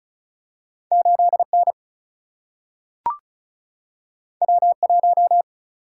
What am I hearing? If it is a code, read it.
Morse code, heard as a keyed tone, about 36 wpm, 700 Hz. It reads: 8 N W 1
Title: Single-Letter Single-Number 35wpm